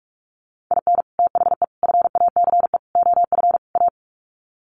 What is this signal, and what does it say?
Morse code, heard as a keyed tone, about 44 wpm, 700 Hz. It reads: IN THE FACE OF A
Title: Sets of 5 Words Encore 45wpm